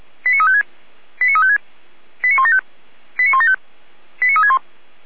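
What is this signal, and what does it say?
Five-tone selective call (ZVEI), heard as quick runs of five tones, tones of about 70 ms. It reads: 89356 89356 89175 89175 89361
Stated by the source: ZVEI five-tone call received on SDR.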